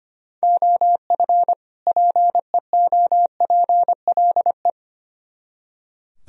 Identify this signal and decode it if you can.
Morse code, heard as a keyed tone, about 33 wpm, 700 Hz. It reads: OF PEOPLE